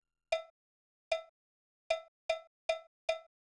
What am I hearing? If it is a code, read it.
There is no code.